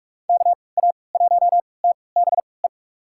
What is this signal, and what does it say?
Morse code, heard as a keyed tone, about 46 wpm, spaced as for 26 wpm, 700 Hz. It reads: KA1TBE
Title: USA Call Signs - 6 Characters - 3x Character Spacing - 45wpm